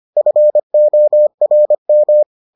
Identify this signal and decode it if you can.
Morse code, heard as a keyed tone, about 25 wpm, 600 Hz. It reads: FORM